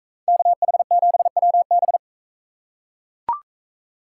Morse code, heard as a keyed tone, about 44 wpm, 700 Hz. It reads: KH7WB